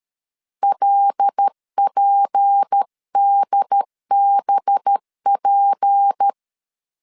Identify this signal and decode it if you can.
Morse code, heard as a keyed tone, about 13 wpm, 800 Hz. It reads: LPDBP
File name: LEZIONI / 6 PKX0 / 6 P 12wpm 2033